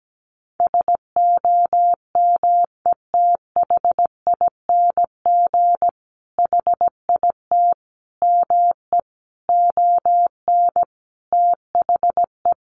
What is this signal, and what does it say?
Morse code, heard as a keyed tone, about 17 wpm, 700 Hz. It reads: SOMETHING HIT ME ON THE
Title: Sentences from Top 1200 Words 17wpm